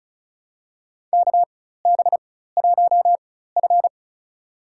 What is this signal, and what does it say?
Morse code, heard as a keyed tone, about 36 wpm, spaced as for 16 wpm, 700 Hz. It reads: KB1F
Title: USA Call Signs - 4 Characters - 4x Character Spacing - 35wpm